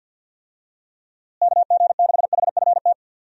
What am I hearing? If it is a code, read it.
Morse code, heard as a keyed tone, about 52 wpm, 700 Hz. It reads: KG6HFT
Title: USA Call Signs - 6 Characters - Repeated 3x - 50wpm